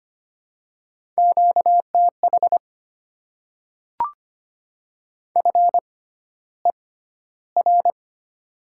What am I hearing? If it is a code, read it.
Morse code, heard as a keyed tone, about 25 wpm, spaced as for 13 wpm, 700 Hz. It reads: QTH FER
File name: Top 20 QSO Elements - 6x Character Spacing - 25wpm